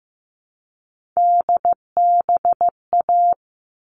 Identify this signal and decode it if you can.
Morse code, heard as a keyed tone, about 15 wpm, 700 Hz. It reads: DBA
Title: All Cut Numbers - 3-Digit 15wpm